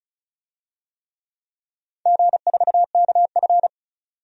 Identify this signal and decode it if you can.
Morse code, heard as a keyed tone, about 36 wpm, 700 Hz. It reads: G4KF
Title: DXCC-100 Call Signs - 4 Characters - 35wpm